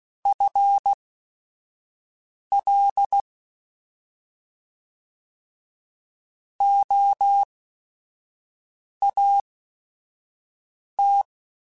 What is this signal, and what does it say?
Morse code, heard as a keyed tone, about 16 wpm, spaced as for 5 wpm, 800 Hz. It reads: FL OAT
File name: FL / FL mp3s / Fast FL @5